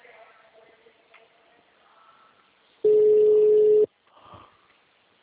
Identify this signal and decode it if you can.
Ringback tone (Europe, 425 Hz).
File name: out- -2016- - - .10407